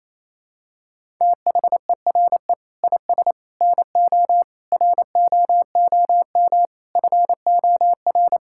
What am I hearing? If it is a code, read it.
Morse code, heard as a keyed tone, about 28 wpm, 700 Hz. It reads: THERE IS NO ROOM FOR
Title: Sentences from Top 1000 Words - Review Part 5 - 28wpm